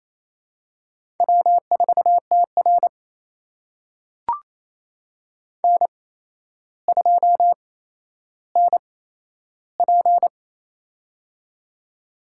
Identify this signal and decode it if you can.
Morse code, heard as a keyed tone, about 28 wpm, spaced as for 12 wpm, 700 Hz. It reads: W4TR N2NP
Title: USA Call Signs - 4 Characters - 8x Character Spacing - 28wpm